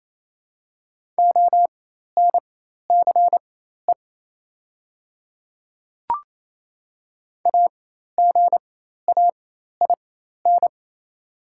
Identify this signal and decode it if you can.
Morse code, heard as a keyed tone, about 28 wpm, spaced as for 13 wpm, 690 Hz. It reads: ONCE AGAIN
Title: 4x Character Spacing - Top 300 Words - 28wpm